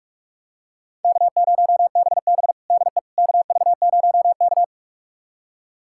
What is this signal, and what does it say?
Morse code, heard as a keyed tone, about 47 wpm, 700 Hz. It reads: K0BB DE KV0K
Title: USA Callsign DE Callsign - 4 Characters - 45wpm